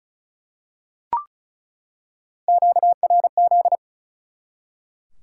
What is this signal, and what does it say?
Morse code, heard as a keyed tone, about 34 wpm, 700 Hz. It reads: QRZ
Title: Top 180 QSO Elements 35wpm